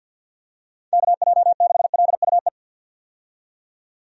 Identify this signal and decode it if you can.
Morse code, heard as a keyed tone, about 52 wpm, 700 Hz. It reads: KJ6LUE
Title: USA Call Signs - 6 Characters - 50wpm